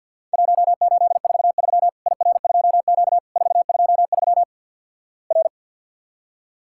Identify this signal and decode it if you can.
Morse code, heard as a keyed tone, about 50 wpm, 700 Hz. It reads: 1843 IR2Q 423 R
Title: Worked All Europe DX Contest - QTC Exchange - 50wpm